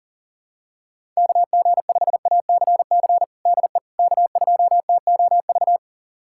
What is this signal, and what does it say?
Morse code, heard as a keyed tone, about 37 wpm, 700 Hz. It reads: KG5ACC DE K2TOV
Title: USA Callsign DE Callsign 40wpm